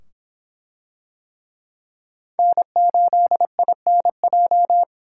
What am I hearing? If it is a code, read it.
Morse code, heard as a keyed tone, about 24 wpm, 700 Hz. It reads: N8INJ